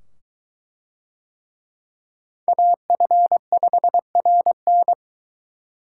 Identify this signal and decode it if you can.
Morse code, heard as a keyed tone, about 22 wpm, 700 Hz. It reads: AF5RN